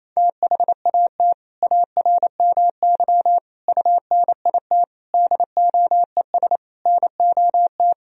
Morse code, heard as a keyed tone, about 28 wpm, 700 Hz. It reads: THAT ARMY UNIT DOES NOT
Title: Sentences from Top 1000 Words - Review Part 2 - 28wpm